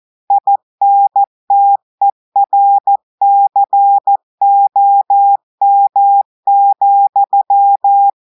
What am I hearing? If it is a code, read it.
Morse code, heard as a keyed tone, about 14 wpm, 800 Hz. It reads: INTERCOM,
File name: audios / portugais / portugais texte en clair 1 vitesse 14 mots minutes